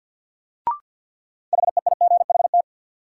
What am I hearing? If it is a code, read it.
Morse code, heard as a keyed tone, about 53 wpm, 700 Hz. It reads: HEIGHT
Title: Sentences from Top 1600 Words - New Word Drill - 50wpm